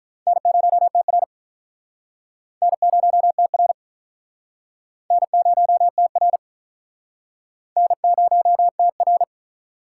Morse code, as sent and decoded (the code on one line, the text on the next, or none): -. ----- - .-. / -. ----- - .-. / -. ----- - .-. / -. ----- - .-.
N0TR N0TR N0TR N0TR